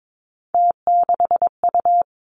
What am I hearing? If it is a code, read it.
Morse code, heard as a keyed tone, about 22 wpm, 700 Hz. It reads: T6U
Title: Three Challenging Letter-Number Combinations 22wpm